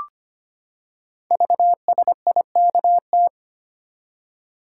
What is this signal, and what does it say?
Morse code, heard as a keyed tone, about 25 wpm, 700 Hz. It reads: VSIKT
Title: Code Groups - 5 Letters - 25wpm